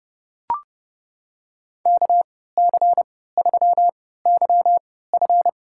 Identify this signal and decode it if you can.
Morse code, heard as a keyed tone, about 31 wpm, spaced as for 17 wpm, 700 Hz. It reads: KC3YF
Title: USA Call Signs - 5 Characters - 3x Character Spacing - 30wpm